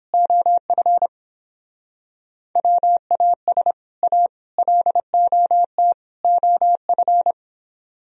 Morse code, heard as a keyed tone, about 30 wpm, 700 Hz. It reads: OF WAS A LOT OF